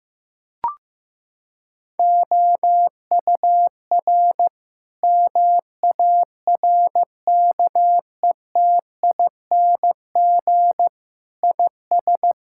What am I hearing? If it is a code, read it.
Morse code, heard as a keyed tone, about 15 wpm, 700 Hz. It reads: OUR MARKETING IS
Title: Sentences from Top 1000 Words - Review Part 5 - 15wpm